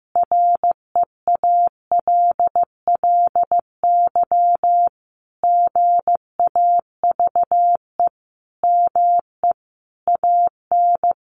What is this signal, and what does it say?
Morse code, heard as a keyed tone, about 15 wpm, 700 Hz. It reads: REALLY GAVE ME AN
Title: Sentences from Top 900 Words 15wpm